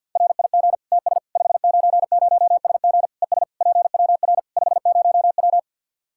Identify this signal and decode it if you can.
Morse code, heard as a keyed tone, about 52 wpm, 700 Hz. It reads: RIG TS 590SG ES PWR 50W
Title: K1USN SST QSO Practice 50wpm